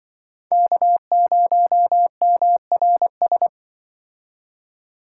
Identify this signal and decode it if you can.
Morse code, heard as a keyed tone, about 24 wpm, 700 Hz. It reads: K0MRS
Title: Speed Racing - USA Call Signs 20wpm